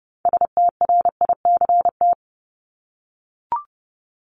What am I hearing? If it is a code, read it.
Morse code, heard as a keyed tone, about 30 wpm, 700 Hz. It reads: STRICT
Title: Top 1501-1750 Words 30wpm